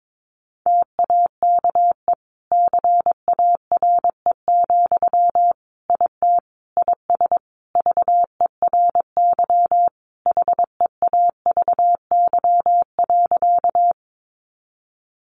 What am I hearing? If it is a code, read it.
Morse code, heard as a keyed tone, about 22 wpm, 700 Hz. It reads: TAKE CARE, IT IS VERY HEAVY.
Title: Sentences from Top 1100 Words 22wpm